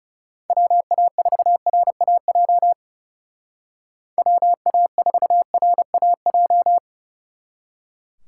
Morse code, heard as a keyed tone, about 37 wpm, 700 Hz. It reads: WA4RAJ WA4RAJ
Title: Speed Racing - USA Call Signs 30wpm